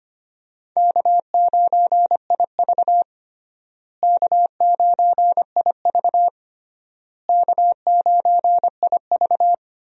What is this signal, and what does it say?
Morse code, heard as a keyed tone, about 25 wpm, 700 Hz. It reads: K9IV K9IV K9IV
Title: USA Call Signs - 4 Characters - Repeated 3x - 25wpm